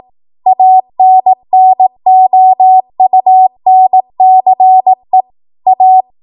Morse code, heard as a keyed tone, about 18 wpm, 750 Hz. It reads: ANNOUNCE A
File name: Archive / 18 WPM / 18WPM